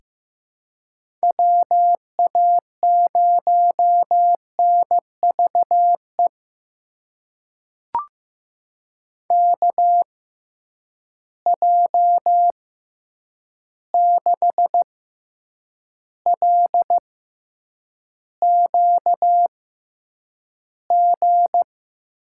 Morse code, heard as a keyed tone, about 15 wpm, spaced as for 8 wpm, 700 Hz. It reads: WA0NVE KJ6LQG
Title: USA Call Signs - 6 Characters - 6x Character Spacing - 15wpm